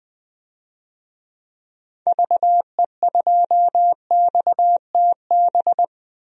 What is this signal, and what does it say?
Morse code, heard as a keyed tone, about 20 wpm, 700 Hz. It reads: VE2XTB
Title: DXCC-100 Call Signs - 6 Characters - 20wpm